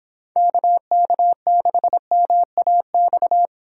Morse code, heard as a keyed tone, about 26 wpm, 700 Hz. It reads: KK6MAX
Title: Speed Racing - USA Call Signs 22wpm